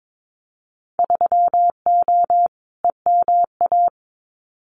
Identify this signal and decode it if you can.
Morse code, heard as a keyed tone, about 22 wpm, 700 Hz. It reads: 3O EMA
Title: Winter Field Day Exchanges 22wpm